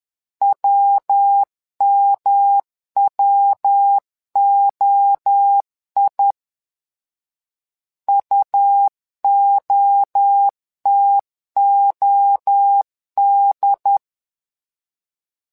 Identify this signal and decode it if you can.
Morse code, heard as a keyed tone, about 10 wpm, 800 Hz. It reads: WMWOI UOTOD
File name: LEZIONI / 3 MWRO / 3 MWRO 10wpm 2148